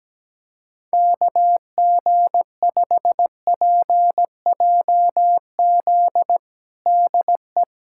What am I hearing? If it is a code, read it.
Morse code, heard as a keyed tone, about 17 wpm, 700 Hz. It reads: KG5PJZ DE
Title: USA Callsign DE Callsign - 6 Characters - 17wpm